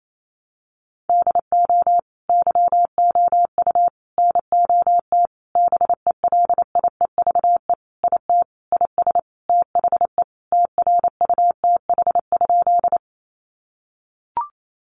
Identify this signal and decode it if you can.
Morse code, heard as a keyed tone, about 28 wpm, 700 Hz. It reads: DO YOU NOT BELIEVE IT IS THE TRUTH?
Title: Sentences from Top 1000 Words 28wpm